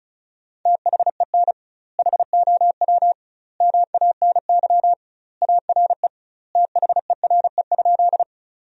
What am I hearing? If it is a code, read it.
Morse code, heard as a keyed tone, about 35 wpm, 700 Hz. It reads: THEN HOW MANY ARE THERE?
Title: 2x Word-Spacing - Sentences from Top 100 Words - 35wpm